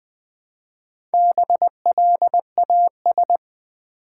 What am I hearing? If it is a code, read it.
Morse code, heard as a keyed tone, about 20 wpm, 690 Hz. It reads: BLAS
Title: Build A Word - 1 Letter At A Time - 5-Letter Words - 20wpm